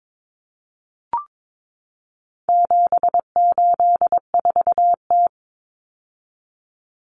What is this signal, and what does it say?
Morse code, heard as a keyed tone, about 22 wpm, 700 Hz. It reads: 784T
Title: Common Cut Numbers - 4-Digit 22wpm